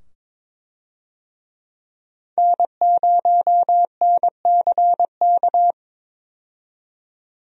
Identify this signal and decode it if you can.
Morse code, heard as a keyed tone, about 22 wpm, 700 Hz. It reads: N0NCK
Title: USA Call Signs - 5 Characters - 22wpm